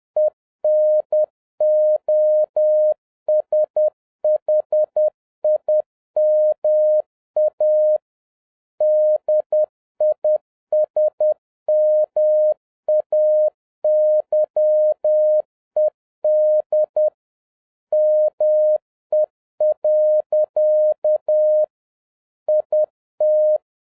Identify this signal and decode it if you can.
Morse code, heard as a keyed tone, about 10 wpm, 610 Hz. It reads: ENOSHIMA DISMAYED ME. IT
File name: 2011-07-mp3 / n31 THE 3.11 2 10wpm